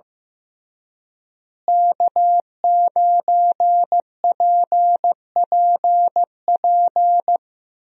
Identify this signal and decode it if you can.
Morse code, heard as a keyed tone, about 15 wpm, 700 Hz. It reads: K9PPP
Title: USA Call Signs - 5 Characters - Repeated 3x - 15wpm